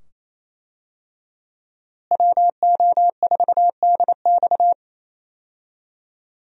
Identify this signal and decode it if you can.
Morse code, heard as a keyed tone, about 28 wpm, 700 Hz. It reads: WO4DX
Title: USA Call Signs - 5 Characters - Repeated 2x - 28wpm